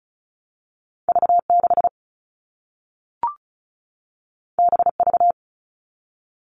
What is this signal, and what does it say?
Morse code, heard as a keyed tone, about 36 wpm, 700 Hz. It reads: V6 BV
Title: All Cut Numbers - 2-Digit 35wpm